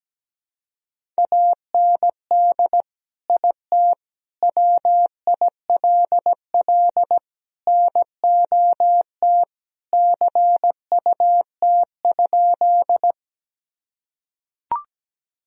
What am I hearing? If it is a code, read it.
Morse code, heard as a keyed tone, about 17 wpm, 700 Hz. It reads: AND IT WILL NOT CUT?
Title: Sentences from Top 1000 Words - Review Part 1 - 17wpm